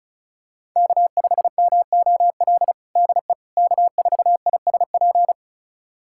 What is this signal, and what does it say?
Morse code, heard as a keyed tone, about 36 wpm, 700 Hz. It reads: K5MOL DE K4ISP
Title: USA Callsign DE Callsign - 5 Characters - 35wpm